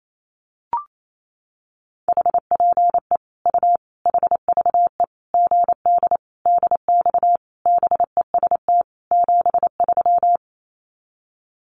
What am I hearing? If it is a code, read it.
Morse code, heard as a keyed tone, about 28 wpm, 700 Hz. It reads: HPE U HVE GD DX BEST 73
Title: Common QSO Phrases 28wpm